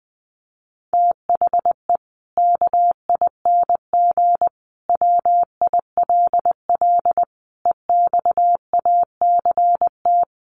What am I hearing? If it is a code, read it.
Morse code, heard as a keyed tone, about 20 wpm, 700 Hz. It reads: THE KING WILL EXACT
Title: Sentences from Top 1300 Words 20wpm